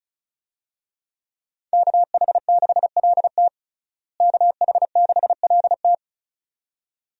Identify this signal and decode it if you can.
Morse code, heard as a keyed tone, about 36 wpm, 700 Hz. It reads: KH6LT KH6LT
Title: USA Call Signs - 5 Characters - Repeated 2x - 35wpm